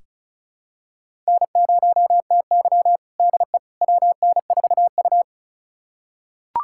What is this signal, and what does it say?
Morse code, heard as a keyed tone, about 34 wpm, 700 Hz. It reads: N0TY DE WN4U
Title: USA Callsign DE Callsign - 4 Characters - 35wpm